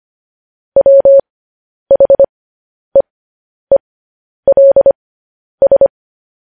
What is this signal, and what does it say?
Morse code, heard as a keyed tone, about 25 wpm, spaced as for 10 wpm, 550 Hz. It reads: WHEELS
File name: podcast / News headlines-1x-25x10